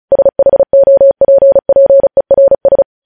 Morse code, heard as a keyed tone, about 36 wpm, 550 Hz. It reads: SHOPPERS
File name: podcast / News headlines-1x-35x35